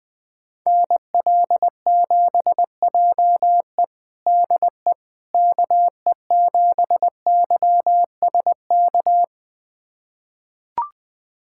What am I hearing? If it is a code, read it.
Morse code, heard as a keyed tone, about 20 wpm, 700 Hz. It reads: NL7JE DE KE7YSK
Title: USA Callsign DE Callsign 20wpm